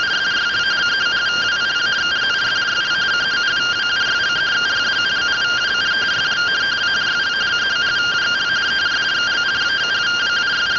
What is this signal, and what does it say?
Two-tone teleprinter signal, about 170 Hz shift.